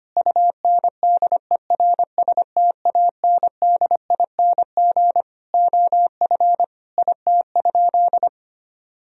Morse code, heard as a keyed tone, about 25 wpm, 700 Hz. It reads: UNDERSTANDING OF IT?
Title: Sentences from Top 1000 Words - Review Part 3 - 25wpm